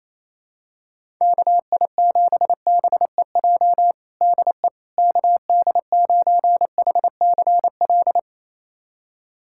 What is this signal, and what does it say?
Morse code, heard as a keyed tone, about 28 wpm, 700 Hz. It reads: KI7BEJ DE KD9HCL
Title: USA Callsign DE Callsign - 6 Characters - 28wpm